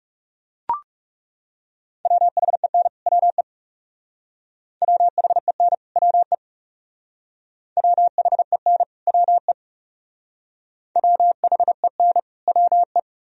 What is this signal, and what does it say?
Morse code, heard as a keyed tone, about 38 wpm, 700 Hz. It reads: WHEN WE WHEN WE WHEN WE WHEN WE